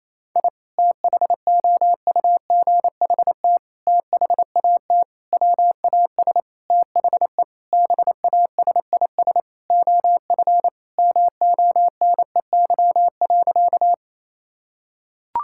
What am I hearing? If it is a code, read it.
Morse code, heard as a keyed tone, about 28 wpm, 700 Hz. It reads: I THOUGHT THAT WAS THE BASIS OF MONEY.
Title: Sentences from Top 1000 Words - Review Part 5 - 28wpm